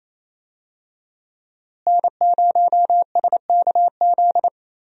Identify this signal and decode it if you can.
Morse code, heard as a keyed tone, about 28 wpm, 700 Hz. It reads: N0SKZ
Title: USA Call Signs - 5 Characters - Repeated 3x - 28wpm